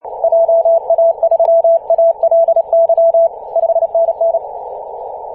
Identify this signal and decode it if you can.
Morse code, heard as a keyed tone, about 30 wpm, 670 Hz, with radio noise. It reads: JA3ALY 5NN
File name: ep2a-21-029mhz-cw